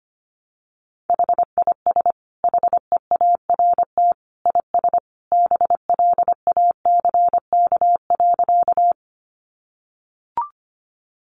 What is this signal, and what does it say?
Morse code, heard as a keyed tone, about 25 wpm, 700 Hz. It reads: HIS HEART IS BLACK.
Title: Sentences from Top 1200 Words 25wpm